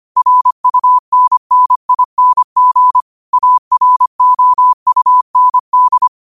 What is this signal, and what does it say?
Morse code, heard as a keyed tone, about 25 wpm, 1000 Hz. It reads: RUNNING AROUND
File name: qotdfiles / 25wpm